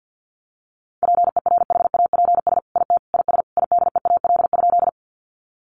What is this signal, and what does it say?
Morse code, heard as a keyed tone, about 50 wpm, 700 Hz. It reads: PERHAPS IT IS IDEAL?
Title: Sentences from Top 1400 Words 50wpm